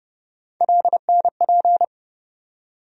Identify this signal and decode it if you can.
Morse code, heard as a keyed tone, about 32 wpm, 700 Hz. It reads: LNP
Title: USA Call Sign Postfixes - 3 Letters 30wpm